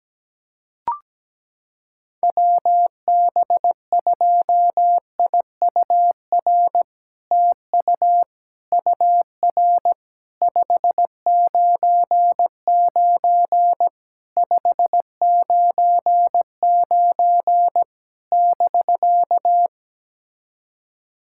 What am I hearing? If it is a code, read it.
Morse code, heard as a keyed tone, about 17 wpm, 700 Hz. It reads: WB2IUR TU UR 599 599 <BK>
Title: SOTA QSO Practice 17wpm